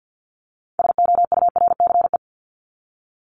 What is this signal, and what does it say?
Morse code, heard as a keyed tone, about 50 wpm, 700 Hz. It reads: SOURCE
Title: ES:S Suffix - Top 100 Words with Base Words 50wpm